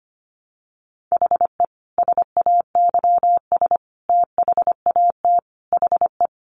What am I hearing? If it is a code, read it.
Morse code, heard as a keyed tone, about 25 wpm, 700 Hz. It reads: HE SAYS THAT HE
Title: Additional Easy Phrases 25wpm